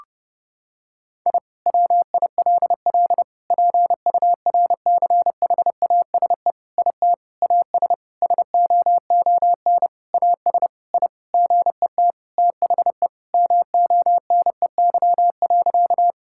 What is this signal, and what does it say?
Morse code, heard as a keyed tone, about 30 wpm, 700 Hz. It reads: I WILL PURCHASE IT AS SOON AS I GET THE MONEY.
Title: Sentences from Top 1000 Words - Review Part 5 - 30wpm